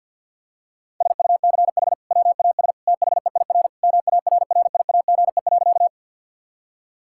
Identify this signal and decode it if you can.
Morse code, heard as a keyed tone, about 50 wpm, 700 Hz. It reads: SUCH WAS THEIR MARRIAGE.
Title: Sentences from Top 1000 Words - Review Part 5 - 50wpm